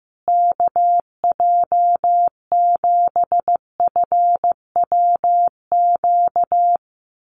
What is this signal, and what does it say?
Morse code, heard as a keyed tone, about 14 wpm, 700 Hz. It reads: KJ7FWQ
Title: USA Call Signs Repeated 2x 15wpm